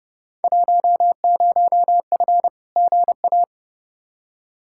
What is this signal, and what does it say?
Morse code, heard as a keyed tone, about 31 wpm, 700 Hz. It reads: 10F GA
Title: Field Day Exchanges Encore - 30wpm